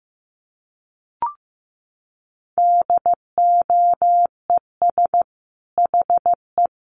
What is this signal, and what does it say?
Morse code, heard as a keyed tone, about 14 wpm, 700 Hz. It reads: DOES HE
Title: Sentences from Top 300 Words 15wpm